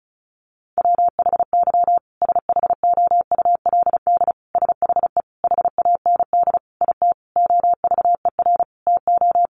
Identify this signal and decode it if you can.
Morse code, heard as a keyed tone, about 34 wpm, 700 Hz. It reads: WHY SHOULD SHE HAND IT OVER TO
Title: Sentences from Top 900 Words 35wpm